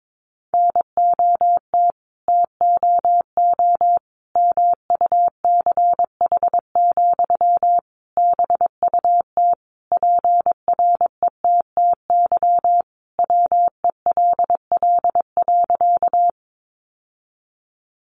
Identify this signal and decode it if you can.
Morse code, heard as a keyed tone, about 22 wpm, 700 Hz. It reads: NOT TOO MUCH, BUT PRETTY WELL.
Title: Sentences from Top 1000 Words 22wpm